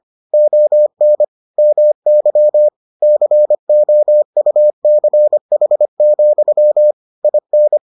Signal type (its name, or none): Morse code